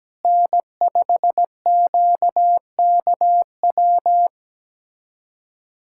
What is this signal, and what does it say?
Morse code, heard as a keyed tone, about 17 wpm, 700 Hz. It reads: N5QKW
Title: USA Callsign DE Callsign - 5 Characters - 17wpm